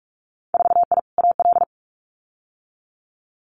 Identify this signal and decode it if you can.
Morse code, heard as a keyed tone, about 44 wpm, 690 Hz. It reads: 4I AL
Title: Winter Field Day Exchanges Encore 45wpm